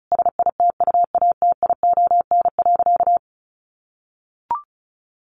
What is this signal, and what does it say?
Morse code, heard as a keyed tone, about 34 wpm, 700 Hz. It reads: SITUATION.
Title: Sentences from Top 1200 Words 35wpm